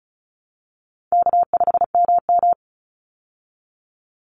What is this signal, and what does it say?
Morse code, heard as a keyed tone, about 34 wpm, 700 Hz. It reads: K5MM
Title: Easy Call Signs Repeated 2x - 35wpm